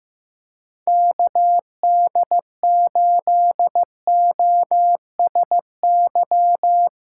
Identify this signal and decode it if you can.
Morse code, heard as a keyed tone, about 15 wpm, 700 Hz. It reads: KD8OSY
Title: USA Call Signs - 6 Characters - 8x Character Spacing - 15wpm